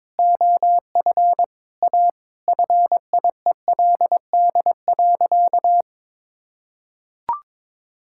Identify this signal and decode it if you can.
Morse code, heard as a keyed tone, about 22 wpm, 700 Hz. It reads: OF A FIELD.
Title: Sentences from Top 1000 Words - Review Part 6 - 22wpm